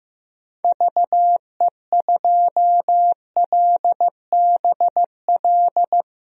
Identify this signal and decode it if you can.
Morse code, heard as a keyed tone, about 15 wpm, 700 Hz. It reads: VE2LBL
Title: DXCC-100 Call Signs - 6 Characters - Repeated 3x - 15wpm